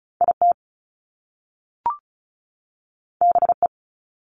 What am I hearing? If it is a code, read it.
Morse code, heard as a keyed tone, about 36 wpm, 700 Hz. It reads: IT BE
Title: Top Two Letter Combinations in Words 35wpm